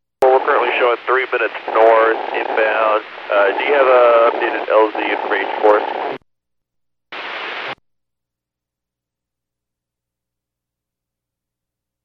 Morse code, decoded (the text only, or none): none